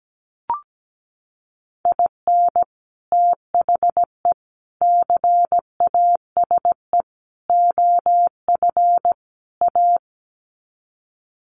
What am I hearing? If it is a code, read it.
Morse code, heard as a keyed tone, about 17 wpm, 700 Hz. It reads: IN THE CASE OF A
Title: Sets of 5 Words Encore 17wpm